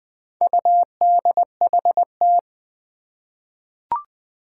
Morse code, heard as a keyed tone, about 20 wpm, 700 Hz. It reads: UDHT
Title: Four Challenging Letter-Number Combinations 20wpm